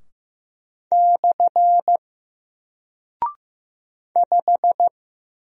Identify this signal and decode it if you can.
Morse code, heard as a keyed tone, about 15 wpm, 700 Hz. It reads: / 5
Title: Single Character (Letter:Number:Punctuation:Prosign) 15wpm